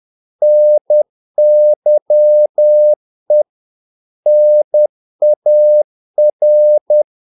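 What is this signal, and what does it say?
Morse code, heard as a keyed tone, about 10 wpm, 600 Hz. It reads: NYE NAR
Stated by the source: Morse practice of short Danish words at 10 wpm, from CW med Gnister.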